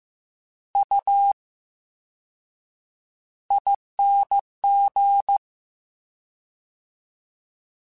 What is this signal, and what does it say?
Morse code, heard as a keyed tone, about 15 wpm, 800 Hz. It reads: U ING
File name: Special INGs / Special INGs-mp3s / Slow ing @13